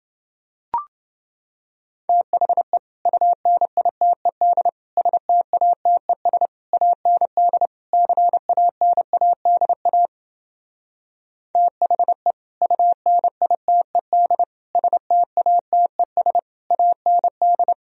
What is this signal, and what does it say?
Morse code, heard as a keyed tone, about 28 wpm, 700 Hz. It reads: THE UNITED STATES AND CANADA THE UNITED STATES AND